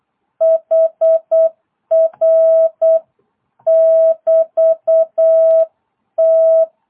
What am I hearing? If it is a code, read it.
Morse code, heard as a keyed tone, about 8 wpm, 660 Hz. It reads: HR<BT>T